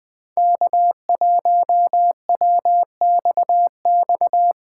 Morse code, heard as a keyed tone, about 21 wpm, 700 Hz. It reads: K1WXX
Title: USA Call Signs - 5 Characters - 3x Character Spacing - 20wpm